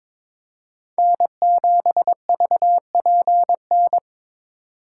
Morse code, heard as a keyed tone, about 22 wpm, 700 Hz. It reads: N7VPN
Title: USA Call Signs - 5 Characters - 4x Character Spacing - 22wpm